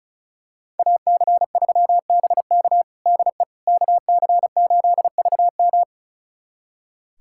Morse code, heard as a keyed tone, about 34 wpm, 700 Hz. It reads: AC3BK DE KC8VM
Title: USA Callsign DE Callsign - 5 Characters - 35wpm